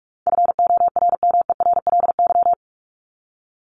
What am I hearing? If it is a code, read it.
Morse code, heard as a keyed tone, about 44 wpm, 700 Hz. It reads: FORMERLY
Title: LY Suffix - Top 100 Words with Base Words 45wpm